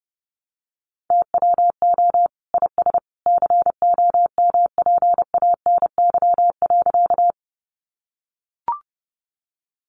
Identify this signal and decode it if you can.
Morse code, heard as a keyed tone, about 30 wpm, 700 Hz. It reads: TWO IS COMPANY.
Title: Sentences from Top 600 Words 30wpm